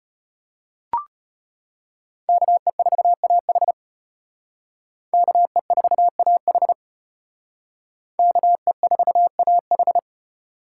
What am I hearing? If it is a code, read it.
Morse code, heard as a keyed tone, about 32 wpm, 700 Hz. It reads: KE4AH KE4AH KE4AH